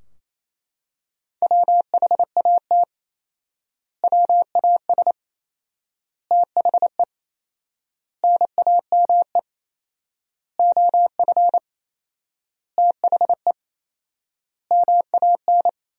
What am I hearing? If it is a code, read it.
Morse code, heard as a keyed tone, about 27 wpm, 700 Hz. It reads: WHAT WAS THE NAME OF THE MAN
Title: 4x Word-Spacing - Sentences from Top 300 Words - 28wpm